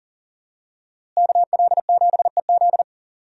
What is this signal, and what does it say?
Morse code, heard as a keyed tone, about 40 wpm, 700 Hz. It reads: KL7EZ
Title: DXCC-100 Call Signs - 5 Characters - Repeated 3x - 40wpm